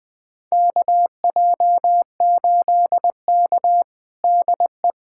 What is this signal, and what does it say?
Morse code, heard as a keyed tone, about 21 wpm, 700 Hz. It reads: KJ8K DE
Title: USA Callsign DE Callsign - 4 Characters - 20wpm